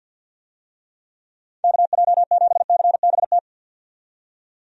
Morse code, heard as a keyed tone, about 51 wpm, 700 Hz. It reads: KJ7CBT
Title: USA Call Signs - 6 Characters - 50wpm